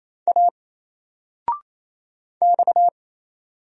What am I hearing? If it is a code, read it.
Morse code, heard as a keyed tone, about 27 wpm, 700 Hz. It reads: A X
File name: Single-Letter-Number-28wpm